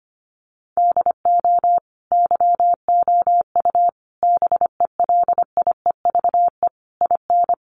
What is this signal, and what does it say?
Morse code, heard as a keyed tone, about 25 wpm, 700 Hz. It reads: DO YOU BELIEVE IN
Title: Sentences from Top 1000 Words 25wpm